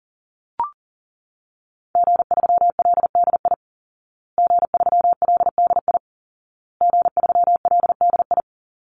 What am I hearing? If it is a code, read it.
Morse code, heard as a keyed tone, about 40 wpm, 700 Hz. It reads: G3LDI G3LDI G3LDI
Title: Call Signs Repeated 2x - 40wpm